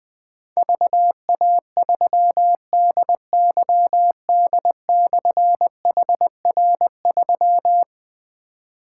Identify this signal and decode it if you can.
Morse code, heard as a keyed tone, about 20 wpm, 700 Hz. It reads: VA3DYD/HR3
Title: Difficult Call Signs 20wpm